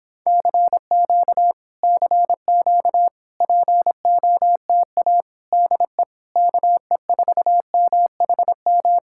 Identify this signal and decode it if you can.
Morse code, heard as a keyed tone, about 26 wpm, 700 Hz. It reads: CQ CQ POTA DE KE4MHM